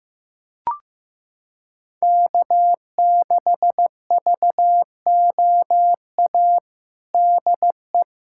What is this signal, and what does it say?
Morse code, heard as a keyed tone, about 15 wpm, 700 Hz. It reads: K6VOA DE
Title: USA Callsign DE Callsign - 5 Characters - 15wpm